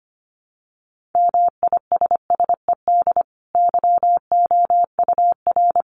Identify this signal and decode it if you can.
Morse code, heard as a keyed tone, about 25 wpm, 700 Hz. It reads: MISSED YOUR
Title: Common QSO Phrases 25wpm